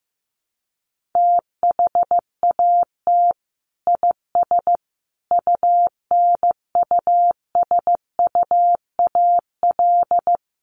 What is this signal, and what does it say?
Morse code, heard as a keyed tone, about 15 wpm, 700 Hz. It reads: THAT IS UNUSUAL
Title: Sentences from Top 1400 Words 15wpm